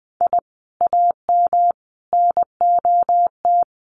Morse code, heard as a keyed tone, about 20 wpm, 700 Hz. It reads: I AM NOT
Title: Sentences from Top 600 Words 20wpm